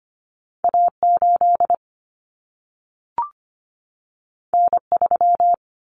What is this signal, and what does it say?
Morse code, heard as a keyed tone, about 25 wpm, 700 Hz. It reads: A8 N3
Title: Common Cut Numbers - 2-Digit 25wpm